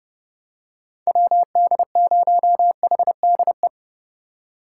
Morse code, heard as a keyed tone, about 32 wpm, 700 Hz. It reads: WD0HDE
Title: USA Call Signs - 6 Characters - 6x Character Spacing - 30wpm